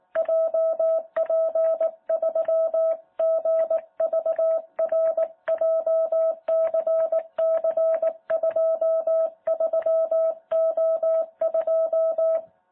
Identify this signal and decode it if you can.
Morse code, heard as a keyed tone, about 18 wpm, 650 Hz. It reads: JP3GVR JCC23O2
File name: jh8cbh / 2021cw